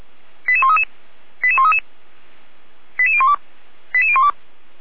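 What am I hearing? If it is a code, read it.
Five-tone selective call (ZVEI), heard as quick runs of five tones, tones of about 70 ms. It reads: 88133 88133 88012 88012
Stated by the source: ZVEI five-tone call received on an SDR.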